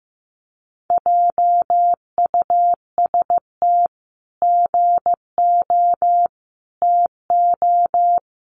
Morse code, heard as a keyed tone, about 15 wpm, 700 Hz. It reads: JUST GO TO
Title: Sentences from Top 500 Words 15wpm